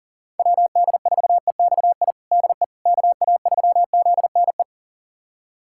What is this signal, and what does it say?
Morse code, heard as a keyed tone, about 40 wpm, 700 Hz. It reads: WD4EXI DE KA3ZNE
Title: USA Callsign DE Callsign - 6 Characters - 40wpm